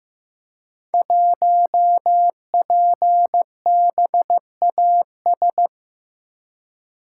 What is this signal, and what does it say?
Morse code, heard as a keyed tone, about 15 wpm, 700 Hz. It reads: 1PBAS
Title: Code Groups - 5 Letter:Numbers - 15wpm